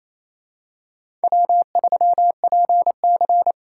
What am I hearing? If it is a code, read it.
Morse code, heard as a keyed tone, about 28 wpm, 700 Hz. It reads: W3PC
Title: USA Call Signs - 4 Characters - 8x Character Spacing - 28wpm